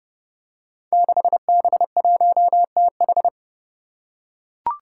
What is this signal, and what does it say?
Morse code, heard as a keyed tone, about 30 wpm, 700 Hz. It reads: 6B1TH
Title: Five Challenging Letter-Number Combinations 30wpm